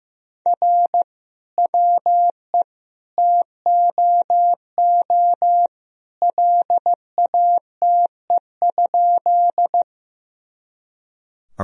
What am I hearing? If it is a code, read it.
Morse code, heard as a keyed tone, about 15 wpm, 700 Hz. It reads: R WE TOO LATE?
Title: Sentences from Top 300 Words and Abbreviations 15wpm